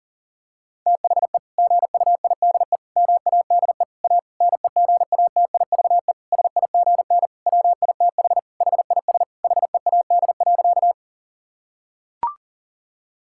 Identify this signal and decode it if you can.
Morse code, heard as a keyed tone, about 40 wpm, 700 Hz. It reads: THE GUIDE MADE A NEGATIVE SIGN WITH HIS HEAD.
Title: Sentences from Top 1000 Words - Review Part 6 - 40wpm